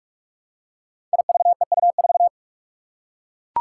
Morse code, heard as a keyed tone, about 43 wpm, 700 Hz. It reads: IVEU4
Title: Five Challenging Letter-Number Combinations 45wpm